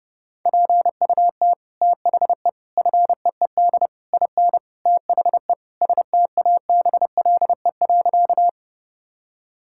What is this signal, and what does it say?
Morse code, heard as a keyed tone, about 32 wpm, 700 Hz. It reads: PUT THE FEED IN THE STABLE.
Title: Sentences from Top 1600 Words 30wpm